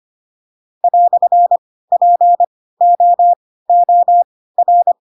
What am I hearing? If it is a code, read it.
Morse code, heard as a keyed tone, about 25 wpm, spaced as for 16 wpm, 700 Hz. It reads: "POOR
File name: Pinocchio / Pinocchio ch4 10 25 ik1hge